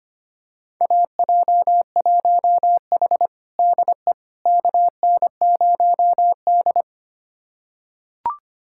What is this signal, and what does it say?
Morse code, heard as a keyed tone, about 25 wpm, 700 Hz. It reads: AJ1H DE KN0D
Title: USA Callsign DE Callsign - 4 Characters - 25wpm